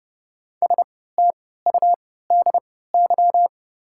Morse code, heard as a keyed tone, about 30 wpm, spaced as for 17 wpm, 700 Hz. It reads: STUDY
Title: 3x Character Spacing - Top 300 Words - 30wpm